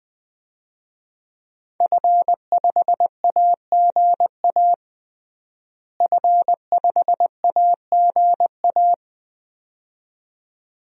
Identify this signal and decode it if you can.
Morse code, heard as a keyed tone, about 20 wpm, 700 Hz. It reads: F5AGA F5AGA
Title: DXCC-100 Call Signs - 5 Characters - Repeated 2x - 20wpm